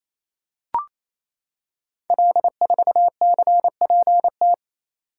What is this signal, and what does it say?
Morse code, heard as a keyed tone, about 28 wpm, 700 Hz. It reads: L4CPT
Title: Code Groups - 5 Letter:Numbers - 28wpm